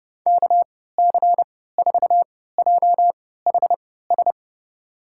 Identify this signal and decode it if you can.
Morse code, heard as a keyed tone, about 32 wpm, spaced as for 17 wpm, 700 Hz. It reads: KC4JHS
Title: USA Call Signs - 6 Characters - 3x Character Spacing - 30wpm